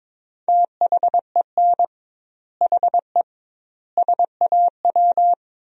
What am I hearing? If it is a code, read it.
Morse code, heard as a keyed tone, about 22 wpm, 700 Hz. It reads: THEN HE SAW
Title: 2x Word-Spacing - Sentences from Top 400 Words - 22wpm